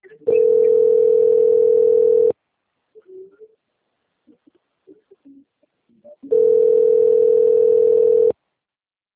Ringback tone (North America), 440 + 480 Hz.
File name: out- - - - - .14859